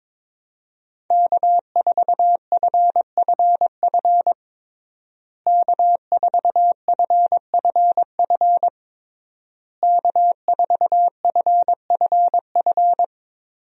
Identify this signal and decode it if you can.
Morse code, heard as a keyed tone, about 22 wpm, 700 Hz. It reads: K4FFF K4FFF K4FFF
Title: USA Call Signs - 5 Characters - Repeated 3x - 22wpm